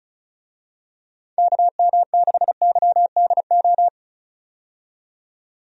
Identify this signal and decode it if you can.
Morse code, heard as a keyed tone, about 34 wpm, 700 Hz. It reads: KM6YDO
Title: USA Call Signs - 6 Characters - 35wpm